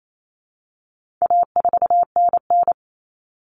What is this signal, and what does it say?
Morse code, heard as a keyed tone, about 28 wpm, 700 Hz. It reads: A4NN
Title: Common Cut Numbers - 4-Digit 28wpm